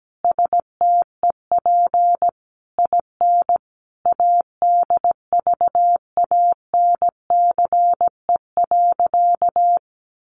Morse code, heard as a keyed tone, about 17 wpm, 700 Hz. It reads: STEP IN ADVANCE.
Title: Sentences from Top 1400 Words 17wpm